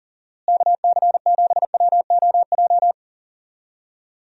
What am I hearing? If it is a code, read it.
Morse code, heard as a keyed tone, about 40 wpm, 700 Hz. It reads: KC7WOJ
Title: USA Call Signs - 6 Characters - Repeated 3x - 40wpm